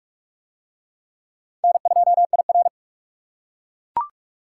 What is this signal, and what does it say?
Morse code, heard as a keyed tone, about 42 wpm, 700 Hz. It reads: N2IR